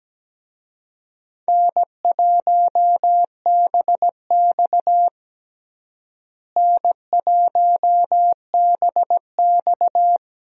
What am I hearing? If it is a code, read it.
Morse code, heard as a keyed tone, about 17 wpm, 700 Hz. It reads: N1BX N1BX
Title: USA Call Signs - 4 Characters - Repeated 3x - 17wpm